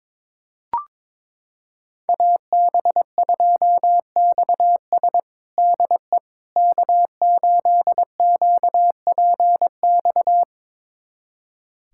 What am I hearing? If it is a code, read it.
Morse code, heard as a keyed tone, about 22 wpm, 700 Hz. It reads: AB2XS DE K8QPX
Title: USA Callsign DE Callsign - 5 Characters - 22wpm